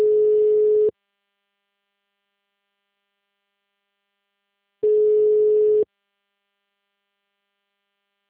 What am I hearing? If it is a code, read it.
Ringback tone (Europe, 425 Hz).